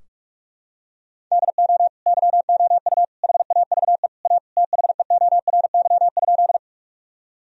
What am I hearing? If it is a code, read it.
Morse code, heard as a keyed tone, about 42 wpm, 700 Hz. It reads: DO YOU HAVE A THEORY?
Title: Sentences from Top 1000 Words - Review Part 3 - 45wpm